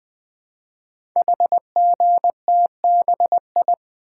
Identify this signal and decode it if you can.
Morse code, heard as a keyed tone, about 20 wpm, 700 Hz. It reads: HGTBI
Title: Code Groups - 5 Letters - 20wpm